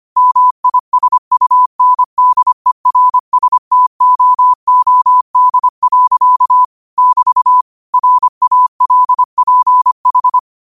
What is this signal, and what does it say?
Morse code, heard as a keyed tone, about 25 wpm, 1000 Hz. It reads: MISUNDERSTOOD. <BT> RALPH
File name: qotdfiles / 25wpm